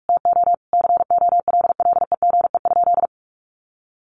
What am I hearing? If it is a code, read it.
Morse code, heard as a keyed tone, about 44 wpm, 700 Hz. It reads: TO COLLEGE?
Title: Sentences from Top 1300 Words 45wpm